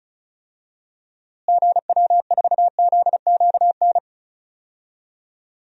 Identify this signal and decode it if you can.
Morse code, heard as a keyed tone, about 36 wpm, 700 Hz. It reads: GW4ZQN
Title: DXCC-100 Call Signs - 6 Characters - Repeated 3x - 35wpm